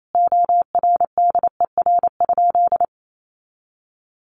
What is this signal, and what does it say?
Morse code, heard as a keyed tone, about 28 wpm, 700 Hz. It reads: ORDER?
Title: Sentences from Top 400 Words 28wpm